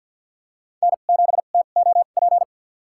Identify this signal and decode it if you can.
Morse code, heard as a keyed tone, about 52 wpm, spaced as for 36 wpm, 700 Hz. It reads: N7TOP
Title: USA Call Signs - 2x Character Spacing - 50wpm